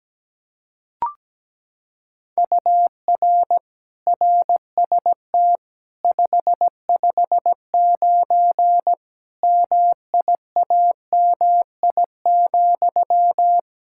Morse code, heard as a keyed tone, about 17 wpm, 700 Hz. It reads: UR RST 559 MIAMI,
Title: K1USN SST QSO Practice 17wpm